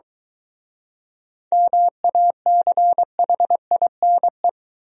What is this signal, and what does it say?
Morse code, heard as a keyed tone, about 23 wpm, 700 Hz. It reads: MACHINE